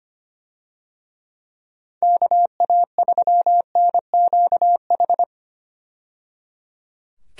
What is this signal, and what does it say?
Morse code, heard as a keyed tone, about 25 wpm, 700 Hz. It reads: KA3NQH
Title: USA Call Signs - 6 Characters - 25wpm